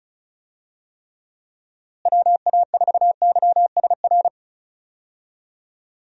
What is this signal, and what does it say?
Morse code, heard as a keyed tone, about 34 wpm, 700 Hz. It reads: WA4YSR
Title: USA Call Signs - 6 Characters - Repeated 3x - 35wpm